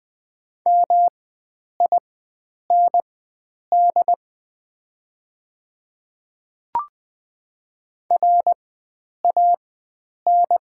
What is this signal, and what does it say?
Morse code, heard as a keyed tone, about 20 wpm, spaced as for 9 wpm, 700 Hz. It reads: MIND RAN
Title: 4x Character Spacing - Top 500 Words - 20wpm